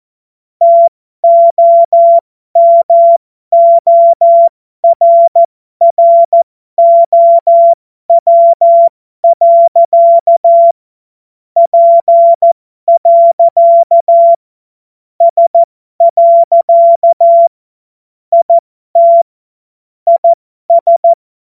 Morse code, heard as a keyed tone, about 14 wpm, spaced as for 12 wpm, 680 Hz. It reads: TOMORROW. P. S. IT IS
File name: Morsekurs / Witze